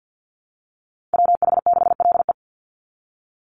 Morse code, heard as a keyed tone, about 49 wpm, 700 Hz. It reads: WH6LE
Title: NA SOTA Chaser Callsigns 50wpm